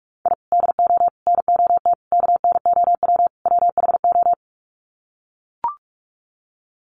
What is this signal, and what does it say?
Morse code, heard as a keyed tone, about 44 wpm, 700 Hz. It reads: I DO NOT KNOW WHO
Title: Sets of 5 Words 45wpm